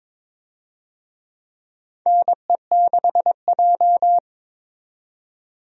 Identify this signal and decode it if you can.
Morse code, heard as a keyed tone, about 22 wpm, 700 Hz. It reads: NE6J
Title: USA Call Signs - 4 Characters - 22wpm